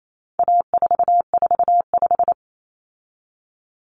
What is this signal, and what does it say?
Morse code, heard as a keyed tone, about 28 wpm, 700 Hz. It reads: A445
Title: Common Cut Numbers - 4-Digit 28wpm